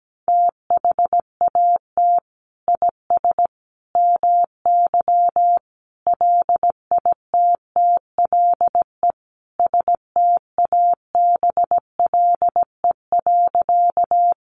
Morse code, heard as a keyed tone, about 17 wpm, 700 Hz. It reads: THAT IS MY LITTLE STABLE.
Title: Sentences from Top 1500 Words 17wpm